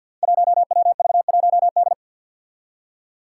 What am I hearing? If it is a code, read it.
Morse code, heard as a keyed tone, about 50 wpm, 700 Hz. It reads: 1WV1D
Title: Five Challenging Letter-Number Combinations 50wpm